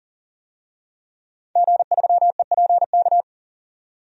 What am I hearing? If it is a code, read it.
Morse code, heard as a keyed tone, about 40 wpm, 700 Hz. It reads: G3EPK
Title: DXCC-100 Call Signs - 5 Characters - 40wpm